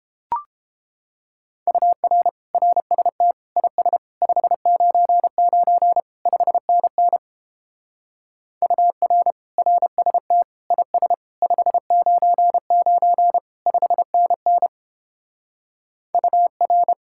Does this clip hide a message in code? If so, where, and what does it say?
Yes: Morse code from 1.7 to 16.9 s, heard as a keyed tone, about 29 wpm, 700 Hz. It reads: UR RST IS 599 5NN UR RST IS 599 5NN UR